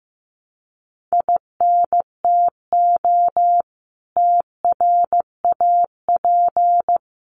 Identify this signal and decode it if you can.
Morse code, heard as a keyed tone, about 15 wpm, 700 Hz. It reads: INTO TRAP
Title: Common QSO Phrases 15wpm